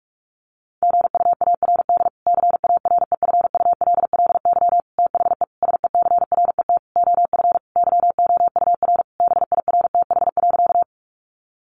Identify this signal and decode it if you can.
Morse code, heard as a keyed tone, about 44 wpm, 700 Hz. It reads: GUARD CAREFULLY THE SECRET OF YOUR BIRTH.
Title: Sentences from Top 1300 Words 45wpm